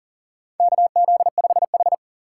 Morse code, heard as a keyed tone, about 40 wpm, 700 Hz. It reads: KZ5H
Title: USA Call Signs - 4 Characters - 4x Character Spacing - 40wpm